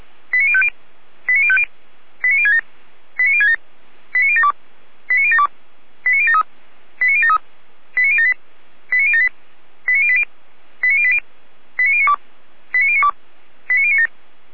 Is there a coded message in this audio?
Yes: five-tone selective call (ZVEI) from 0.3 to 14.1 s, heard as quick runs of five tones, tones of about 70 ms. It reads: 89050 89050 89067 89067 89072 89072 89073 89073 89078 89078 89080 89080 89092 89092 89097